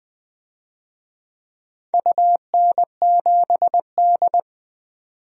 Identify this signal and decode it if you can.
Morse code, heard as a keyed tone, about 20 wpm, 700 Hz. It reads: UN7D
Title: DXCC-100 Call Signs - 4 Characters - 20wpm